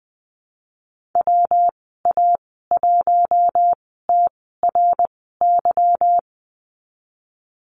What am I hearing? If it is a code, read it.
Morse code, heard as a keyed tone, about 19 wpm, spaced as for 14 wpm, 700 Hz. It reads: WA1TRY
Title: 147 Lesson 24g - Letter y - Call Signs 20wpm